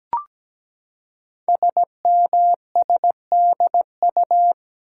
Morse code, heard as a keyed tone, about 17 wpm, 700 Hz. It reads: SMSDU
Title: Code Groups - 5 Letters - 17wpm